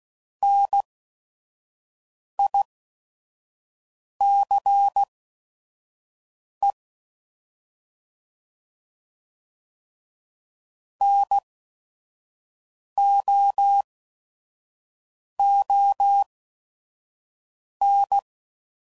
Morse code, heard as a keyed tone, about 16 wpm, spaced as for 5 wpm, 800 Hz. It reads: NICE NOON